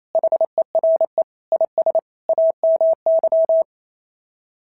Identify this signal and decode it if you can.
Morse code, heard as a keyed tone, about 27 wpm, 660 Hz. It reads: HERE IS AMY
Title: SOTA QSO Practice 28wpm